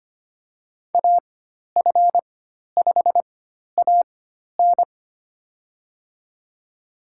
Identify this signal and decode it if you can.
Morse code, heard as a keyed tone, about 25 wpm, spaced as for 12 wpm, 700 Hz. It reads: AF5AN
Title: USA Call Signs - 5 Characters - 4x Character Spacing - 25wpm